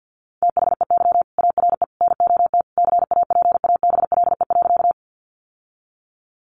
Morse code, heard as a keyed tone, about 50 wpm, 700 Hz. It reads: THEY ARE NOT CAPABLE.
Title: Sentences from Top 1200 Words 50wpm